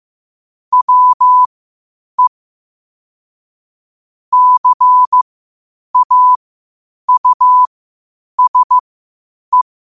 Morse code, heard as a keyed tone, about 15 wpm, spaced as for 8 wpm, 1000 Hz. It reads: WE CAUSE
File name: qotdfiles / 8wpm